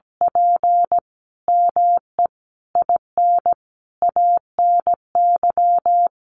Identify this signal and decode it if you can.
Morse code, heard as a keyed tone, about 17 wpm, 700 Hz. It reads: P ME IN ANY
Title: Sentences from Top 600 Words 17wpm